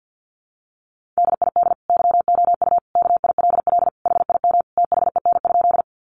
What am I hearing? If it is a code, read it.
Morse code, heard as a keyed tone, about 50 wpm, 700 Hz. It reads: DID YOU KILL HIM THEN?
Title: Sentences from Top 1400 Words 50wpm